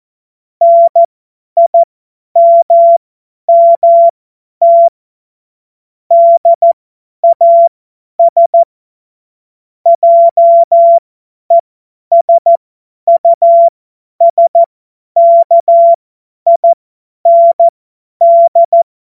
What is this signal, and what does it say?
Morse code, heard as a keyed tone, about 14 wpm, spaced as for 10 wpm, 680 Hz. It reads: NIMMT DAS JESUSKIND
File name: Morsekurs / Witze / Witz 02. November 2021, 50 F70